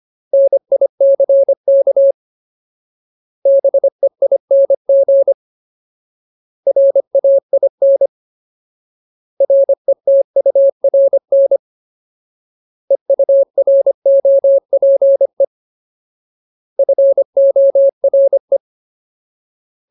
Morse code, heard as a keyed tone, about 25 wpm, 550 Hz. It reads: NICK BEING RAIN RETURN EUROPE FORE
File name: Ultimate Word List / 25WPM Ultimate